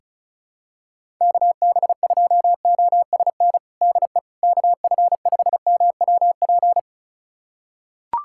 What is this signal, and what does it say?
Morse code, heard as a keyed tone, about 34 wpm, 700 Hz. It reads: KB2OSN DE KF5MWP
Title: USA Callsign DE Callsign 35wpm